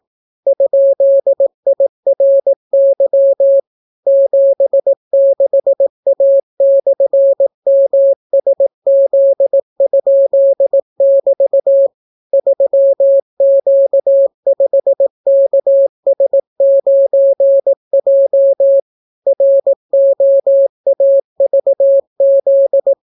Morse code, heard as a keyed tone, about 18 wpm, 550 Hz. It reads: ?IRY 76A/MSZ?<BT> 3Q5KS9J ROAVZ